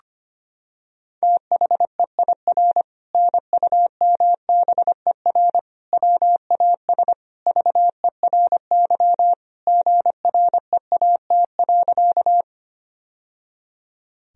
Morse code, heard as a keyed tone, about 25 wpm, 700 Hz. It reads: THEIR NUMBER WAS VERY GREAT.
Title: Sentences from Top 1000 Words - Review Part 1 - 25wpm